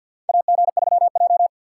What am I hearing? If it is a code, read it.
Morse code, heard as a keyed tone, about 50 wpm, 700 Hz. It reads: AG3J
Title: USA Call Signs - 4 Characters - 6x Character Spacing - 50wpm